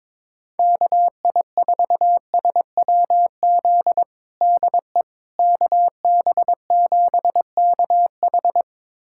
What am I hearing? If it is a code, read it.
Morse code, heard as a keyed tone, about 21 wpm, 700 Hz. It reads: KI4SWZ DE KB7KH
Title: USA Callsign DE Callsign 22wpm